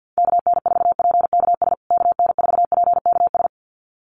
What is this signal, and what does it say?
Morse code, heard as a keyed tone, about 50 wpm, 700 Hz. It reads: KN4PKS KN4PKS
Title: USA Call Signs Repeated 2x 50wpm